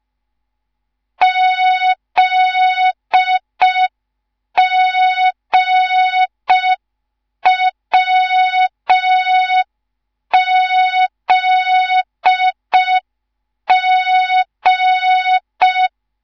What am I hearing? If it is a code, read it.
Morse code, heard as a keyed tone, about 5 wpm, 3700 Hz. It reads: ZGWZG